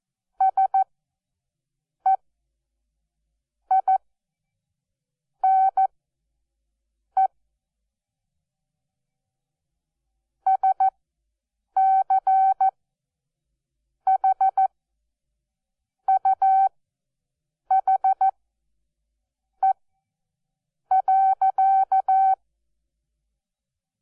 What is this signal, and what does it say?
Morse code, heard as a keyed tone, about 14 wpm, spaced as for 6 wpm, 770 Hz. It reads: SEINE SCHUHE.
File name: Onlinelehrgang / morsen / DJ4UF Morsen12